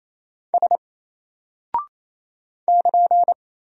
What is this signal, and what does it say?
Morse code, heard as a keyed tone, about 27 wpm, 700 Hz. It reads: S <KN>
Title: Single Character (Letter:Number:Punctuation:Prosign) Alt 28wpm